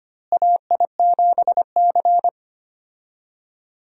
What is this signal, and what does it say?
Morse code, heard as a keyed tone, about 25 wpm, 700 Hz. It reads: AI7C
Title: USA Call Signs - 4 Characters - Repeated 2x - 25wpm